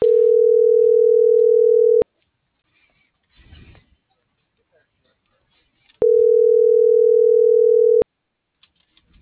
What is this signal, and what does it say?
Ringback tone (North America), 440 + 480 Hz.